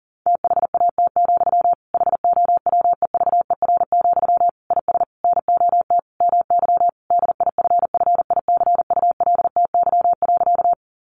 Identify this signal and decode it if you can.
Morse code, heard as a keyed tone, about 40 wpm, 700 Hz. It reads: THAT, HOWEVER, IS NOT MY DIFFICULTY.
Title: Sentences from Top 1500 Words 40wpm